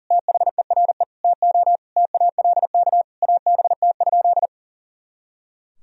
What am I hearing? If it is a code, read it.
Morse code, heard as a keyed tone, about 40 wpm, 700 Hz. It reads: THERE TO TALK ABT?
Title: Sentences from Top 400 Words and Abbreviations 40wpm